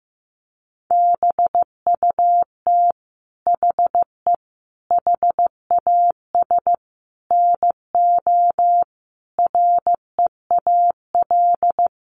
Morse code, heard as a keyed tone, about 15 wpm, 700 Hz. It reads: BUT HE HAS NO REAL
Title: Sentences from Top 1300 Words 15wpm